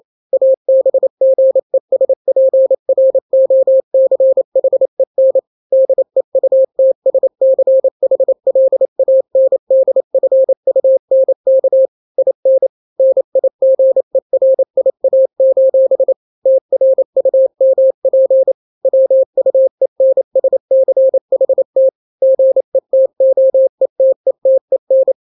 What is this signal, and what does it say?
Morse code, heard as a keyed tone, about 28 wpm, 530 Hz. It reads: ABGESPROCHEN DEUTSCHLANDFUNK IN NIGERIA: TRUMP WUENSCHT GETOETETEN
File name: daily / 27wpm